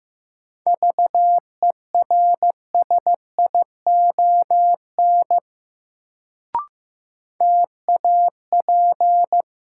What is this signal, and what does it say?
Morse code, heard as a keyed tone, about 15 wpm, 700 Hz. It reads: VERSION TAP
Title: Sentences from Top 1400 Words - New Word Drill - 15wpm